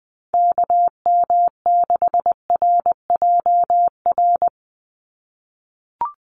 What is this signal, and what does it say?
Morse code, heard as a keyed tone, about 20 wpm, 700 Hz. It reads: KM6RJR
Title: USA Call Signs 20wpm